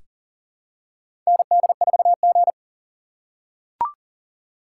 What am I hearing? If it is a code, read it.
Morse code, heard as a keyed tone, about 40 wpm, 700 Hz. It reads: ND4G
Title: K1USN SST QSO Practice - 40wpm